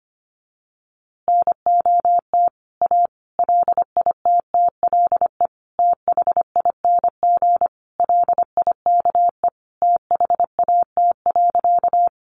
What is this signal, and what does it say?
Morse code, heard as a keyed tone, about 25 wpm, 700 Hz. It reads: NOT A LITTLE THING LIKE THAT.
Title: A Sound Of Thunder 25wpm